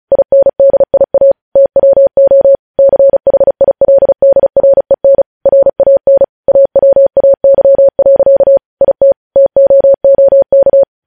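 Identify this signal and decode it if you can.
Morse code, heard as a keyed tone, about 36 wpm, 550 Hz. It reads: INDIA TWO CHILDREN RAN AWAY. IT TOOK
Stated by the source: Morse code news headlines sent at 35 wpm.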